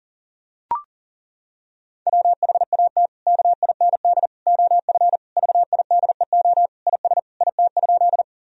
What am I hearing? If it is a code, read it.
Morse code, heard as a keyed tone, about 40 wpm, 700 Hz. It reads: WHAT KIND OF VIDEO IS IT?
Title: Sentences from Top 1000 Words - Review Part 4 - 40wpm